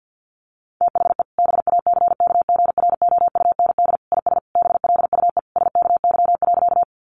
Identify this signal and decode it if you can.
Morse code, heard as a keyed tone, about 50 wpm, 700 Hz. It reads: THE BACKGROUND IS BLUE SKY.
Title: Sentences from Top 1300 Words 50wpm